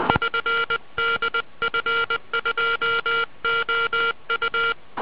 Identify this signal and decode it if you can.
Morse code, heard as a keyed tone, about 20 wpm, 1390 Hz. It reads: <SN> DF2OU